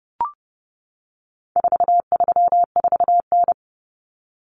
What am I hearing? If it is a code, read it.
Morse code, heard as a keyed tone, about 30 wpm, 700 Hz. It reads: 434N
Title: Common Cut Numbers - 4-Digit 30wpm